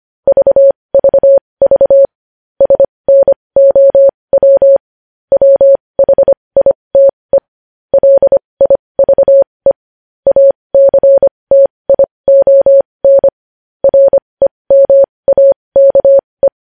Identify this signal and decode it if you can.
Morse code, heard as a keyed tone, about 25 wpm, spaced as for 20 wpm, 550 Hz. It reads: VVV SNOW WHITE LIVE ACTION REMAKE
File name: podcast / News headlines-1x-25x20